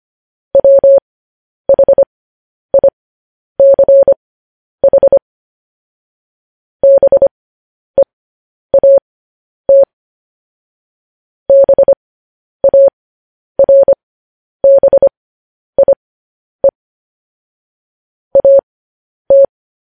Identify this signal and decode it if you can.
Morse code, heard as a keyed tone, about 25 wpm, spaced as for 10 wpm, 550 Hz. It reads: WHICH BEAT BARBIE AT
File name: podcast / News headlines-1x-25x10